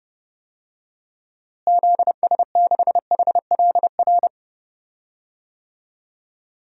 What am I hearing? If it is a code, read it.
Morse code, heard as a keyed tone, about 32 wpm, 700 Hz. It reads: ZS6HLR
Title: DXCC-100 Call Signs - 6 Characters - Repeated 3x - 30wpm